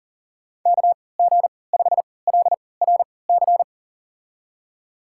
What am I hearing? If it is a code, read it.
Morse code, heard as a keyed tone, about 40 wpm, spaced as for 23 wpm, 700 Hz. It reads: KG5LRC
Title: USA Call Signs - 6 Characters - 3x Character Spacing - 40wpm